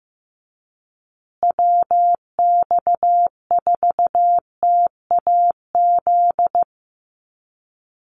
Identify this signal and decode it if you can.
Morse code, heard as a keyed tone, about 15 wpm, 700 Hz. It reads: WX4TAZ
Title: USA Call Signs Repeated 2x 15wpm